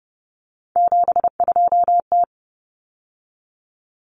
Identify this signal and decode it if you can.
Morse code, heard as a keyed tone, about 30 wpm, 700 Hz. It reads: 72T
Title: Common Cut Numbers - 3-Digit 30wpm